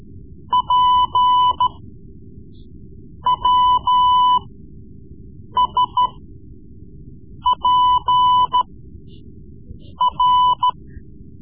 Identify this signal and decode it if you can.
Morse code, heard as a keyed tone, about 14 wpm, spaced as for 5 wpm, 1000 Hz. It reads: PWSPR
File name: Morse Lessons / Morse Lessons / Tape-3B